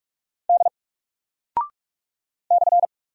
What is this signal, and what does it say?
Morse code, heard as a keyed tone, about 46 wpm, 700 Hz. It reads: D /
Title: Single Character (Letter:Number:Punctuation:Prosign) 45wpm